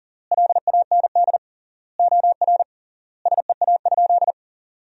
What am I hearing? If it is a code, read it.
Morse code, heard as a keyed tone, about 40 wpm, 700 Hz. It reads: LAND OR SEA?
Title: 3x Word-Spacing - Sentences from Top 500 Words - 40wpm